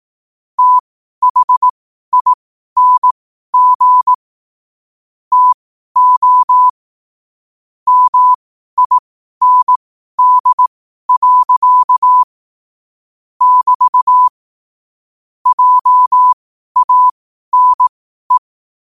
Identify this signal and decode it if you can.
Morse code, heard as a keyed tone, about 18 wpm, spaced as for 12 wpm, 1000 Hz. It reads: THING TO MIND. <BT> JANE
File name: qotdfiles / 12wpm